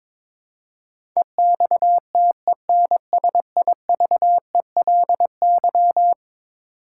Morse code, heard as a keyed tone, about 22 wpm, 700 Hz. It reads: EXTENSIVELY
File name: practice / 22wpm / Eleven Letter Words - 22wpm